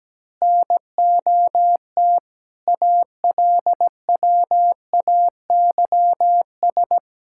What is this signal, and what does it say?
Morse code, heard as a keyed tone, about 17 wpm, 700 Hz. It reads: NOT ALWAYS
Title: Sentences from Top 1000 Words - Review Part 6 - 17wpm